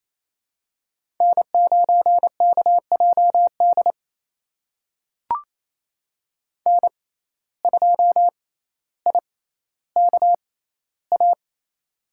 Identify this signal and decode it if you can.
Morse code, heard as a keyed tone, about 28 wpm, spaced as for 14 wpm, 700 Hz. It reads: N9KJD N2IKA
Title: USA Call Signs - 6x Character Spacing - 28wpm